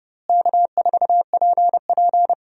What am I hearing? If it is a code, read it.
Morse code, heard as a keyed tone, about 30 wpm, 700 Hz. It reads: K4PP
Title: USA Call Signs - 4 Characters - Repeated 3x - 30wpm